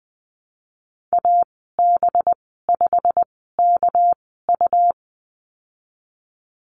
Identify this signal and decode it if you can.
Morse code, heard as a keyed tone, about 19 wpm, spaced as for 14 wpm, 700 Hz. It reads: AB5KU
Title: 219 Lesson 35g - Number 0 - Call Signs 20wpm